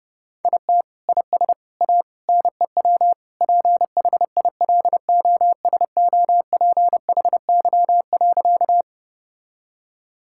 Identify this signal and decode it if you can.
Morse code, heard as a keyed tone, about 30 wpm, 700 Hz. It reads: IT IS A NEW PHILOSOPHY.
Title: Sentences from Top 1000 Words - Review Part 6 - 30wpm